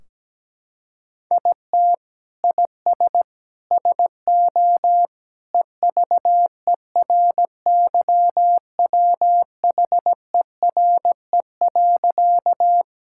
Morse code, heard as a keyed tone, about 17 wpm, 700 Hz. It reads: IT IS SO EVERYWHERE.
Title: Sentences from Top 1600 Words 17wpm